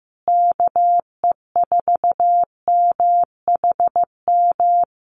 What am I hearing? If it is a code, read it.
Morse code, heard as a keyed tone, about 15 wpm, 700 Hz. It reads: KE4MHM
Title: Common QSO Phrases Encore 15wpm